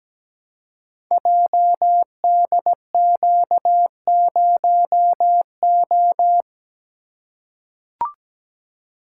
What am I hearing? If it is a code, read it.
Morse code, heard as a keyed tone, about 17 wpm, 690 Hz. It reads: JDQ0O
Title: Code Groups - 5 Letter:Numbers - 17wpm